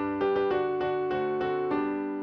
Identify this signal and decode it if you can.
There is no code.